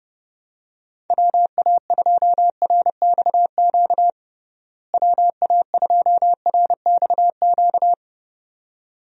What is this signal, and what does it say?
Morse code, heard as a keyed tone, about 30 wpm, 700 Hz. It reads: WA2RXQ WA2RXQ
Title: USA Call Signs - 6 Characters - Repeated 2x - 30wpm